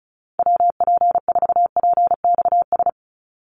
Morse code, heard as a keyed tone, about 34 wpm, 700 Hz. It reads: WP4PXS
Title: USA Call Signs 35wpm